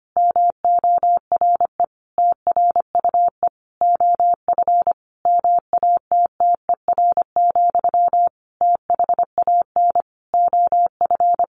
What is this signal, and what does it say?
Morse code, heard as a keyed tone, about 25 wpm, 700 Hz. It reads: MORE TRUE OF MATTER, THAN OF
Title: Sentences from Top 1400 Words 25wpm